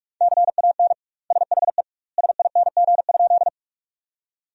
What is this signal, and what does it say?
Morse code, heard as a keyed tone, about 41 wpm, 700 Hz. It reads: CAN SHE SING?
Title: 2x Word-Spacing - Sentences from Top 400 Words - 45wpm